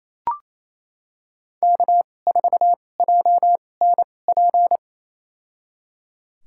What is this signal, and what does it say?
Morse code, heard as a keyed tone, about 27 wpm, spaced as for 20 wpm, 700 Hz. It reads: K4JNP
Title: USA Call Signs - 2x Character Spacing - 28wpm